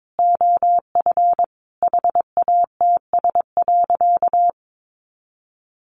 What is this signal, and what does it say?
Morse code, heard as a keyed tone, about 22 wpm, 700 Hz. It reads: OF HATS.
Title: Easy Phrases 22wpm